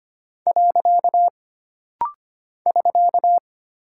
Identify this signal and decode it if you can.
Morse code, heard as a keyed tone, about 25 wpm, 700 Hz. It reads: . <SK>
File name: Single Character (Letter:Number:Punctuation:Prosign) - Rapid-Fire - 25wpm